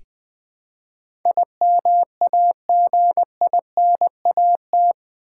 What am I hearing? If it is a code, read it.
Morse code, heard as a keyed tone, about 20 wpm, 700 Hz. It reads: IMAGINAT
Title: Build A Word - 4 Letters At A Time - 11-Letter Words - 20wpm